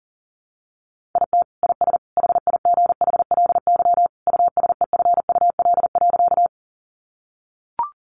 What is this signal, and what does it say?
Morse code, heard as a keyed tone, about 40 wpm, 700 Hz. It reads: IT IS HIGHLY USEFUL.
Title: Sentences from Top 1100 Words 40wpm